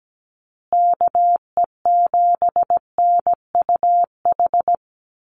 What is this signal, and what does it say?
Morse code, heard as a keyed tone, about 17 wpm, 700 Hz. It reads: KE7NUH
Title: USA Call Signs Repeated 3x 17wpm